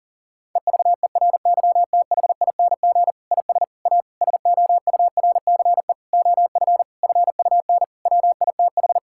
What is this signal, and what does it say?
Morse code, heard as a keyed tone, about 40 wpm, 700 Hz. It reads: EVERYTHING IS A SOURCE OF FUN WITH
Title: Sentences from Top 1000 Words - Review Part 5 - 40wpm